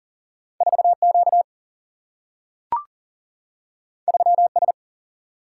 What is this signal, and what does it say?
Morse code, heard as a keyed tone, about 40 wpm, 700 Hz. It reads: 4 Q 3 S
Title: Single-Letter Single-Number 40wpm